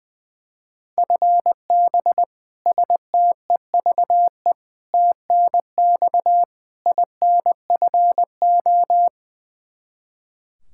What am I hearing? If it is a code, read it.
Morse code, heard as a keyed tone, about 21 wpm, 700 Hz. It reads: FB STEVE TNX INFO
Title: Practice QSO - Exchange 3 - More Info - 20wpm